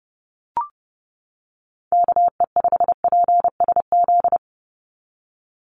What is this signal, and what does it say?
Morse code, heard as a keyed tone, about 31 wpm, 700 Hz. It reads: KE5PSZ
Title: USA Call Signs 30wpm